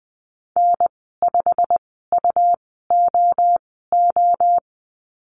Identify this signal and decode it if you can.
Morse code, heard as a keyed tone, about 20 wpm, spaced as for 15 wpm, 700 Hz. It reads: N5UOO
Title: 167 Lesson 27g - Number 7 - Call Signs 20wpm